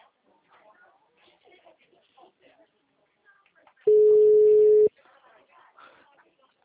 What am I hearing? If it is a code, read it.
Ringback tone (Europe, 425 Hz), with speech.